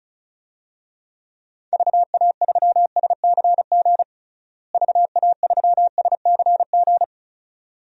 Morse code, heard as a keyed tone, about 34 wpm, 700 Hz. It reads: VA3SCG VA3SCG
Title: DXCC-100 Call Signs - 6 Characters - Repeated 2x - 35wpm